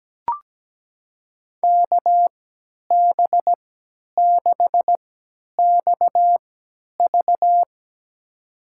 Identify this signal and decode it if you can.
Morse code, heard as a keyed tone, about 17 wpm, spaced as for 10 wpm, 700 Hz. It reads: KB6XV
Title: USA Call Signs - 5 Characters - 3x Character Spacing - 17wpm